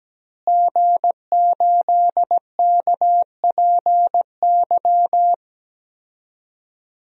Morse code, heard as a keyed tone, about 17 wpm, 700 Hz. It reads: G8KPY
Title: DXCC-100 Call Signs - 5 Characters - Repeated 2x - 17wpm